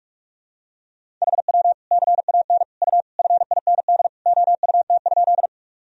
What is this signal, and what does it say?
Morse code, heard as a keyed tone, about 45 wpm, 700 Hz. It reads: HW CAN U FIND OUT?
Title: Sentences from Top 200 Words and Abbreviations 45wpm